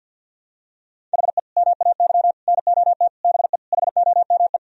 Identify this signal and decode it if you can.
Morse code, heard as a keyed tone, about 46 wpm, 700 Hz. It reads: HE MAY NOT BE HOME
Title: Sentences from Top 1000 Words - Review Part 2 - 50wpm